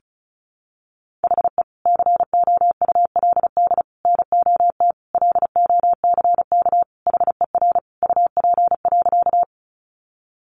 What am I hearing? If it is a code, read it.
Morse code, heard as a keyed tone, about 34 wpm, 700 Hz. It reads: HE COULD NOT LOCK HER UP.
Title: Sentences from Top 1300 Words 35wpm